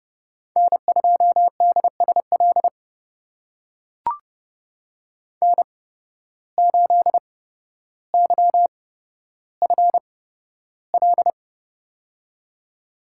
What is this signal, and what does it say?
Morse code, heard as a keyed tone, about 30 wpm, spaced as for 13 wpm, 700 Hz. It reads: N2DSL N8YFL
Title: USA Call Signs - 8x Character Spacing - 30wpm